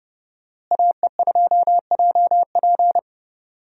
Morse code, heard as a keyed tone, about 30 wpm, 700 Hz. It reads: AE2JP
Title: USA Call Signs - 5 Characters - Repeated 3x - 30wpm